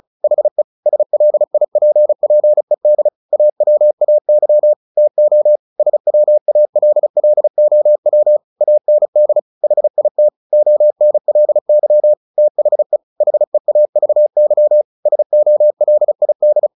Morse code, heard as a keyed tone, about 34 wpm, 600 Hz. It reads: HE SLIPPED AWAY TO SWALLOW AND HIT ONLY THE HEAVY SOLID
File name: the old man and the sea 2 35wpm